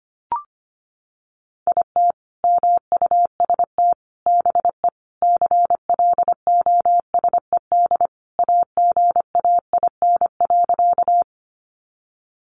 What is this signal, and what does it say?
Morse code, heard as a keyed tone, about 25 wpm, 700 Hz. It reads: IT MUST BE CLOSED AGAIN.
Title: Sentences from Top 1400 Words 25wpm